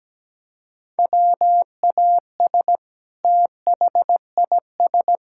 Morse code, heard as a keyed tone, about 17 wpm, 700 Hz. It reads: WAS THIS
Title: Sentences from Top 1600 Words 17wpm